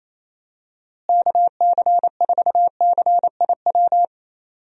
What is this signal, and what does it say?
Morse code, heard as a keyed tone, about 27 wpm, 700 Hz. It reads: KC4CIW
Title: USA Call Signs - 6 Characters - Repeated 2x - 28wpm